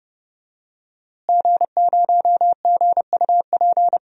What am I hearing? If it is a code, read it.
Morse code, heard as a keyed tone, about 30 wpm, 700 Hz. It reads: G0GUP
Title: DXCC-100 Call Signs - 5 Characters - Repeated 3x - 30wpm